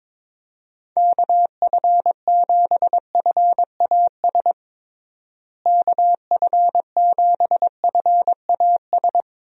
Morse code, heard as a keyed tone, about 22 wpm, 700 Hz. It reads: KF7FAS KF7FAS
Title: USA Call Signs - 6 Characters - Repeated 2x - 22wpm